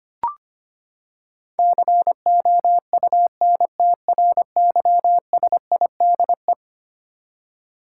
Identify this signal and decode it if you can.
Morse code, heard as a keyed tone, about 25 wpm, 700 Hz. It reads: COUNTRYSIDE
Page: morse-code-ninja-practice-25wpm